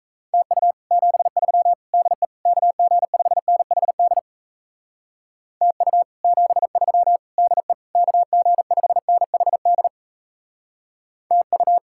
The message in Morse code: TU 73 DE KG5NHD TU 73 DE KG5NHD TU

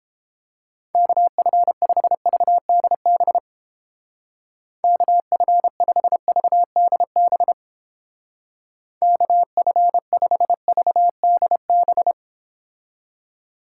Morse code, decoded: KF5VDB KF5VDB KF5VDB

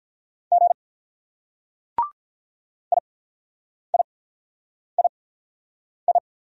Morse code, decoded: G IIII

